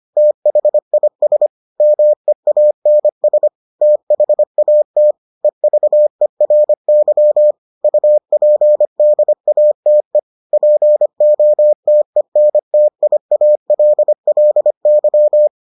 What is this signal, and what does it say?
Morse code, heard as a keyed tone, about 25 wpm, 600 Hz. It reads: THIS MEANS THAT EVERY UPDATE POTENTIALLY